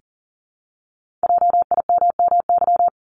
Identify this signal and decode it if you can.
Morse code, heard as a keyed tone, about 40 wpm, 700 Hz. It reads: JIMMY
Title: Multi-letter Phonemes Repeated 3x 40wpm